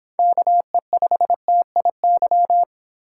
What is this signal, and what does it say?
Morse code, heard as a keyed tone, about 25 wpm, 700 Hz. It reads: KE5TIY